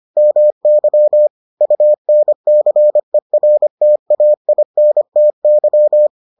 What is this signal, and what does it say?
Morse code, heard as a keyed tone, about 25 wpm, 600 Hz. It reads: MY UNCERTAINTY